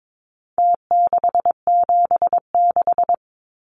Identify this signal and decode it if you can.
Morse code, heard as a keyed tone, about 22 wpm, 700 Hz. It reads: T676
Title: Numbers - Contest Style - 4-Digit 22wpm